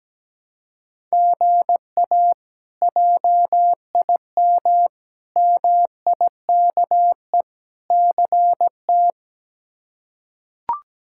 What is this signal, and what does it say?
Morse code, heard as a keyed tone, about 17 wpm, 700 Hz. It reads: GA JIM MIKE CT
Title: K1USN SST QSO Practice - 17wpm